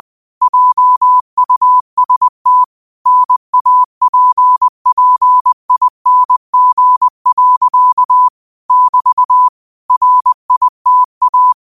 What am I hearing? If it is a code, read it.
Morse code, heard as a keyed tone, about 20 wpm, 1000 Hz. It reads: JUST NAPPING. <BT> RITA